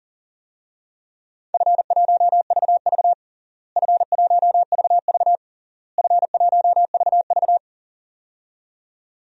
Morse code, heard as a keyed tone, about 40 wpm, 700 Hz. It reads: F1VV F1VV F1VV
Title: DXCC-100 Call Signs - 4 Characters - Repeated 3x - 40wpm